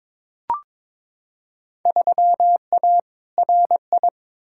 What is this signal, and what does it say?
Morse code, heard as a keyed tone, about 22 wpm, 700 Hz. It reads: 3A RI
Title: Field Day Exchanges - 22wpm